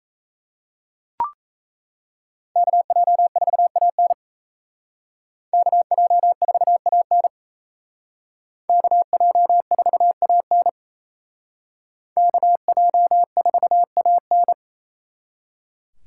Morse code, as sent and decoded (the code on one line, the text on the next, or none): -.- .--- ....- .- -. / -.- .--- ....- .- -. / -.- .--- ....- .- -. / -.- .--- ....- .- -.
KJ4AN KJ4AN KJ4AN KJ4AN